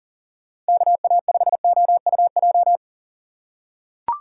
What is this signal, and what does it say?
Morse code, heard as a keyed tone, about 40 wpm, 700 Hz. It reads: KA5OUJ
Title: USA Call Signs - 6 Characters - 2x Character Spacing - 40wpm